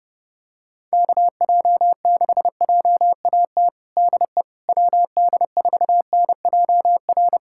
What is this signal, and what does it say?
Morse code, heard as a keyed tone, about 30 wpm, 700 Hz. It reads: KJ6JAT DE WD4NJR
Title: USA Callsign DE Callsign - 6 Characters - 30wpm